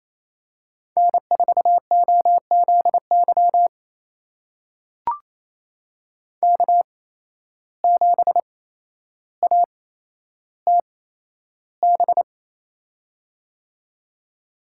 Morse code, heard as a keyed tone, about 28 wpm, spaced as for 12 wpm, 700 Hz. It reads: N4OZY K7ATB
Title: USA Call Signs - 5 Characters - 8x Character Spacing - 28wpm